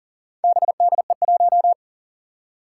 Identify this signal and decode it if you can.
Morse code, heard as a keyed tone, about 40 wpm, 700 Hz. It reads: BDE1
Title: Four Challenging Letter-Number Combinations 40wpm